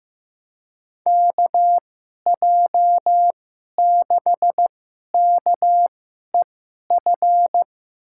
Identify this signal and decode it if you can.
Morse code, heard as a keyed tone, about 15 wpm, spaced as for 11 wpm, 700 Hz. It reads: KJ6KEF
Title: USA Call Signs - 6 Characters - 2x Character Spacing - 15wpm